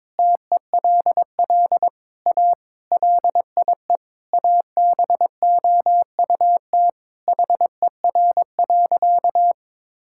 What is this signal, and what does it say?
Morse code, heard as a keyed tone, about 22 wpm, 700 Hz. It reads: TELL A LIE ABOUT HER.
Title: Sentences from Top 1600 Words 22wpm